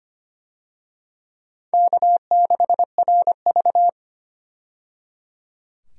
Morse code, heard as a keyed tone, about 25 wpm, 700 Hz. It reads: K6RV
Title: USA Call Signs - 4 Characters - 25wpm